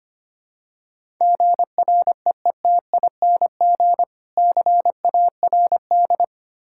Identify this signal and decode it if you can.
Morse code, heard as a keyed tone, about 25 wpm, 700 Hz. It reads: GREETING CARD
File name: practice / 25wpm / Difficult Words with Hints - 25wpm